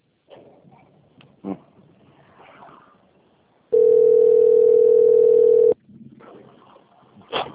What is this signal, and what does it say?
Ringback tone (North America), 440 + 480 Hz.
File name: out- - - - - .2598